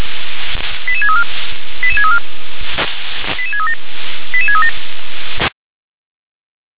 Five-tone selective call (ZVEI), heard as quick runs of five tones, tones of about 70 ms. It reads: 88634 88634 88637 88637
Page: SDR/ZVEI